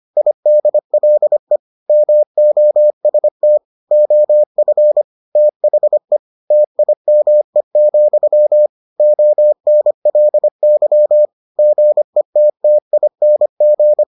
Morse code, heard as a keyed tone, about 25 wpm, 600 Hz. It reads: IDLE MOST OF THE TIME, ONLY GETTING